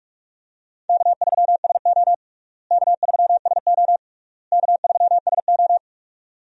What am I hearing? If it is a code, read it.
Morse code, heard as a keyed tone, about 41 wpm, 700 Hz. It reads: K3SO K3SO K3SO
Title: USA Call Signs - 4 Characters - Repeated 3x - 45wpm